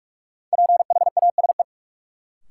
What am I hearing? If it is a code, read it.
Morse code, heard as a keyed tone, about 43 wpm, 700 Hz. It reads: PHASE
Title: Sentences from Top 1200 Words - New Word Drill Alt - 45wpm